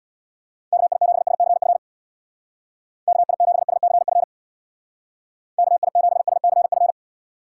Morse code, heard as a keyed tone, about 65 wpm, 700 Hz. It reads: KI7SCF KI7SCF KI7SCF